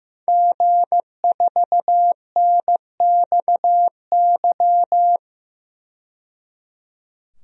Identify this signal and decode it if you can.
Morse code, heard as a keyed tone, about 15 wpm, 700 Hz. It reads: G4NXY
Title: DXCC-100 Call Signs - 5 Characters - 15wpm